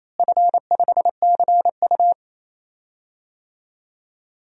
Morse code, heard as a keyed tone, about 27 wpm, 700 Hz. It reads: F5CU
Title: DXCC-100 Call Signs - 4 Characters - Repeated 3x - 28wpm